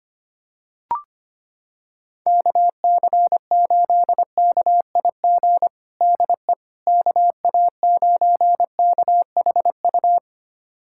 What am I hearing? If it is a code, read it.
Morse code, heard as a keyed tone, about 25 wpm, 700 Hz. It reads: KC8KIG DE KA9KHU
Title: USA Callsign DE Callsign - 6 Characters - 25wpm